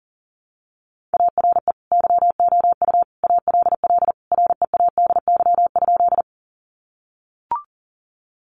Morse code, heard as a keyed tone, about 40 wpm, 700 Hz. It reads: ARE YOU ALL READY?
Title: Sentences from Top 400 Words 40wpm